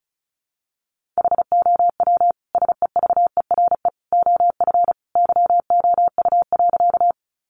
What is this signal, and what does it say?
Morse code, heard as a keyed tone, about 34 wpm, 700 Hz. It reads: HOW SEVERE OF YOU.
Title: Sentences from Top 1400 Words 35wpm